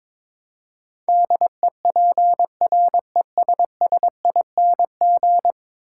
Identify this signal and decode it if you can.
Morse code, heard as a keyed tone, about 22 wpm, 700 Hz. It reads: DEPRESSING
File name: practice / 22wpm / 10 Letter Words - First 3 Letters Given - 22wpm